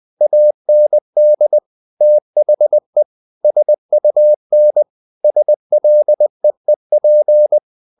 Morse code, heard as a keyed tone, about 20 wpm, 600 Hz. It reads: AND THE SUN SLEEP
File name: the old man and the sea part 2 20wpm